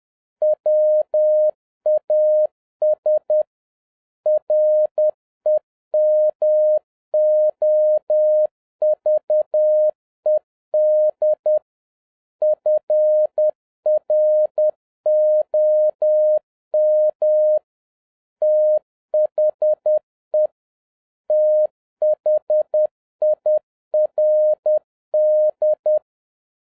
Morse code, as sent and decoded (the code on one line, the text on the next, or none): .-- .- ... / .-. -- --- ...- . -.. / ..-. .-. --- -- / - .... / - .... .. .-. -..
WAS REMOVED FROM THE THIRD